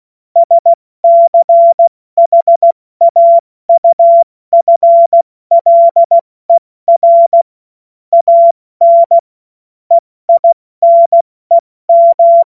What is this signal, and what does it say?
Morse code, heard as a keyed tone, about 16 wpm, 680 Hz. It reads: SCHAUFLER AN EINEM
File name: Morsekurs / Witze / Witz 19 Dezember 2023 70 F80